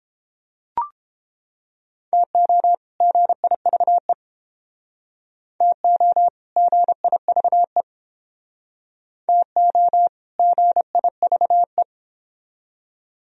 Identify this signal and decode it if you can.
Morse code, heard as a keyed tone, about 31 wpm, 700 Hz. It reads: TO GIVE TO GIVE TO GIVE